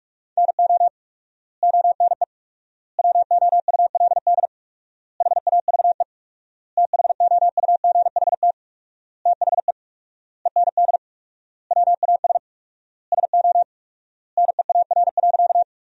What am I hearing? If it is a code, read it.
Morse code, heard as a keyed tone, about 41 wpm, 700 Hz. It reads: NO ONE WOULD HAVE THOUGHT THE END WAS SO NEAR.
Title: 4x Word-Spacing - Sentences from Top 300 Words - 45wpm